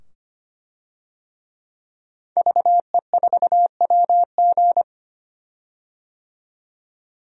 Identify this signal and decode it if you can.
Morse code, heard as a keyed tone, about 25 wpm, 700 Hz. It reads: VE4WG
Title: DXCC-100 Call Signs - 5 Characters - 25wpm